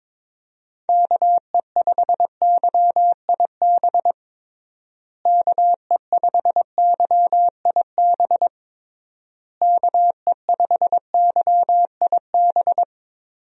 Morse code, heard as a keyed tone, about 22 wpm, 700 Hz. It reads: KE5YIB KE5YIB KE5YIB
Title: USA Call Signs - 6 Characters - Repeated 3x - 22wpm